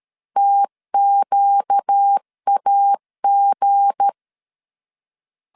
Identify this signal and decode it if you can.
Morse code, heard as a keyed tone, about 13 wpm, 800 Hz. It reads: TQAG
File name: LEZIONI / 7 GF9Z / 7 GF9 12wpm 2319